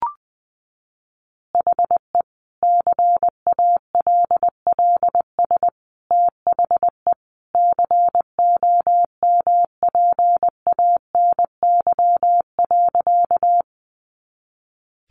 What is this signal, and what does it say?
Morse code, heard as a keyed tone, about 19 wpm, 700 Hz. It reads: HE CALLS THE COMPANY.
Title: Additional Easy Phrases 20wpm